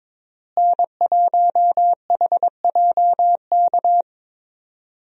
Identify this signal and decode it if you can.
Morse code, heard as a keyed tone, about 21 wpm, 700 Hz. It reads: N1HJK
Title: USA Call Signs - 5 Characters - Repeated 2x - 22wpm